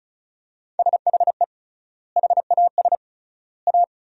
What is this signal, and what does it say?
Morse code, heard as a keyed tone, about 34 wpm, 700 Hz. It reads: SHE HAS A
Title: 3x Word-Spacing - Sentences from Top 200 Words - 35wpm